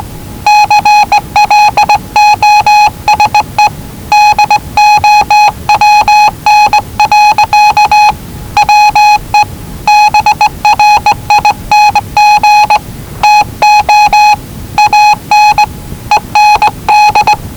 Morse code, heard as a keyed tone, about 19 wpm, 840 Hz. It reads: CLOSE DOWN. WE BRING TO AN END
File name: WellingtonRadioZlwCloseDownOnHfAndMf / 01-Zlw 4277khz 417.5 qrt until 0925z